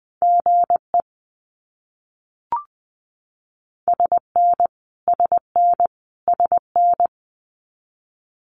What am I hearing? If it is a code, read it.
Morse code, heard as a keyed tone, about 20 wpm, 700 Hz. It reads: GE SN SN SN
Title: Abbreviations Repeated 3x - 20wpm